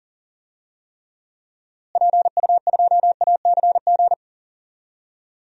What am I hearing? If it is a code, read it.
Morse code, heard as a keyed tone, about 40 wpm, 700 Hz. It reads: PU2ACG
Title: DXCC-100 Call Signs - 6 Characters - Repeated 2x - 40wpm